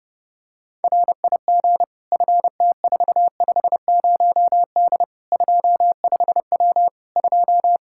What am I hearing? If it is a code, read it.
Morse code, heard as a keyed tone, about 30 wpm, 700 Hz. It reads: RIG FT450D 25W 2
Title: Practice QSO - Exchange 2 - Info - 30wpm